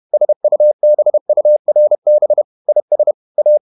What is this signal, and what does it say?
Morse code, heard as a keyed tone, about 31 wpm, 600 Hz. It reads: SUBURB IS A
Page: Never 31wpm | CW med Gnister